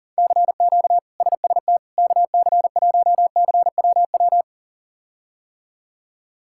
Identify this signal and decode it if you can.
Morse code, heard as a keyed tone, about 40 wpm, 700 Hz. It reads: CQ SST KC1CWW
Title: K1USN SST QSO Practice - 40wpm